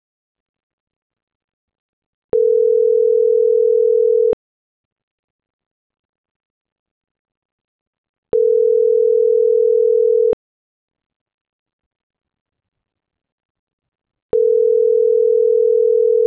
Ringback tone (North America), 440 + 480 Hz.